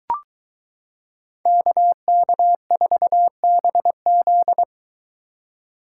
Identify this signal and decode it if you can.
Morse code, heard as a keyed tone, about 23 wpm, 700 Hz. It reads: KK4BZ